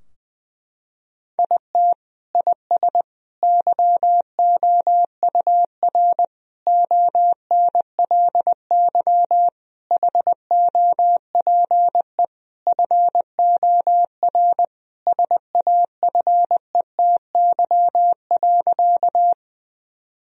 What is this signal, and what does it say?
Morse code, heard as a keyed tone, about 20 wpm, 700 Hz. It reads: IT IS YOUR ONLY HOPE FOR SAFETY.
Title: Sentences from Top 1000 Words - Review Part 4 - 20wpm